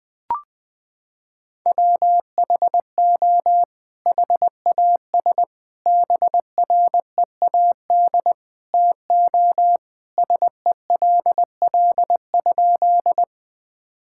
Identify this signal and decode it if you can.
Morse code, heard as a keyed tone, about 21 wpm, 700 Hz. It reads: WHO HAS BREAD TO SELL?
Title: Sentences from Top 1600 Words 20wpm